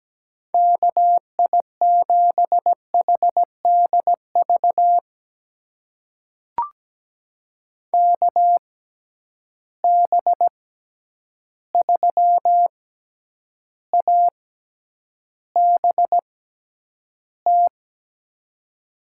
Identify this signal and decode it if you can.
Morse code, heard as a keyed tone, about 17 wpm, spaced as for 9 wpm, 700 Hz. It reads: KI7HDV KB3ABT
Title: USA Call Signs - 6 Characters - 6x Character Spacing - 17wpm